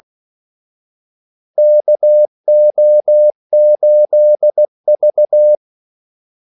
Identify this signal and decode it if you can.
Morse code, heard as a keyed tone, about 16 wpm, 600 Hz. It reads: KO8V
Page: Kaldesignaler 16 wpm